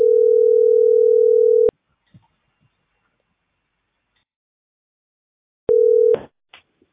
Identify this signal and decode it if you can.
Ringback tone (North America), 440 + 480 Hz.